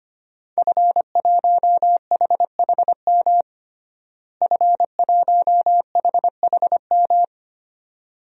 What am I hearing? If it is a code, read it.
Morse code, heard as a keyed tone, about 25 wpm, 700 Hz. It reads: F1HHM F1HHM
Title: DXCC-100 Call Signs - 5 Characters - Repeated 2x - 25wpm